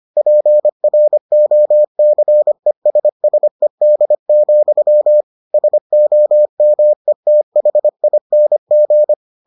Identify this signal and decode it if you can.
Morse code, heard as a keyed tone, about 25 wpm, 600 Hz. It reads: PROCESSED, SOMETHING